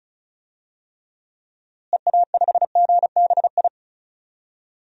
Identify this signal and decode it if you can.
Morse code, heard as a keyed tone, about 34 wpm, 700 Hz. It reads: EA5GBI
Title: DXCC-100 Call Signs - 6 Characters - Repeated 3x - 35wpm